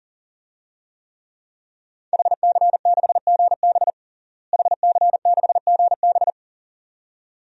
Morse code, heard as a keyed tone, about 40 wpm, 700 Hz. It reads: HC6GB HC6GB
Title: DXCC-100 Call Signs - 5 Characters - Repeated 2x - 40wpm